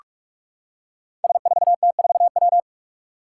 Morse code, heard as a keyed tone, about 44 wpm, 700 Hz. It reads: S4T4W
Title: Five Challenging Letter-Number Combinations 45wpm